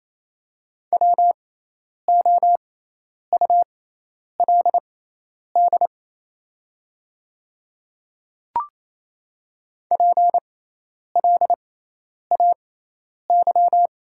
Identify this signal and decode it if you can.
Morse code, heard as a keyed tone, about 28 wpm, spaced as for 10 wpm, 700 Hz. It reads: WOULD PLAY
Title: 6x Character Spacing - Top 500 Words - 28wpm